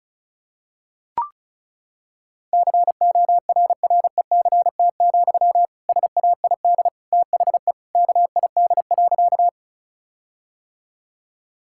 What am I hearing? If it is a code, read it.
Morse code, heard as a keyed tone, about 36 wpm, 700 Hz. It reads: CORRECT, SAID THE KID.
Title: Sentences from Top 1600 Words 35wpm